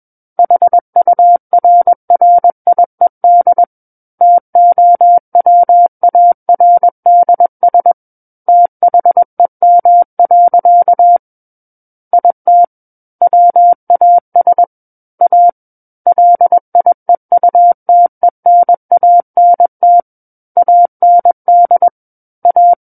Morse code, heard as a keyed tone, about 21 wpm, 700 Hz. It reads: HURRIED TOWARDS THEM. IT WAS A LIEUTENANT AND A
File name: Book One / 12-Chapter 12 - 21 WPM